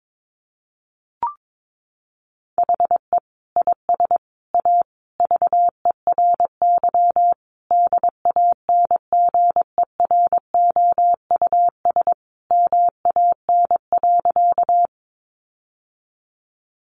Morse code, heard as a keyed tone, about 22 wpm, 700 Hz. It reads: HE IS A VERY DANGEROUS MAN.
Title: Sentences from Top 1200 Words 22wpm